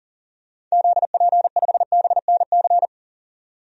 Morse code, heard as a keyed tone, about 40 wpm, 700 Hz. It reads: ZP5BNC
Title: DXCC-100 Call Signs - 6 Characters - 40wpm